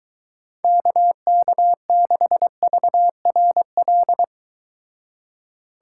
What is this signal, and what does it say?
Morse code, heard as a keyed tone, about 24 wpm, 700 Hz. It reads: KK6VRL